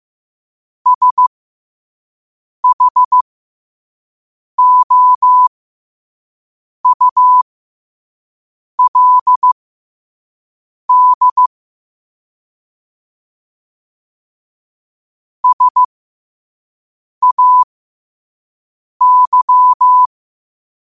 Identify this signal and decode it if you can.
Morse code, heard as a keyed tone, about 15 wpm, spaced as for 5 wpm, 1000 Hz. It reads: SHOULD SAY
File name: qotdfiles / 5wpm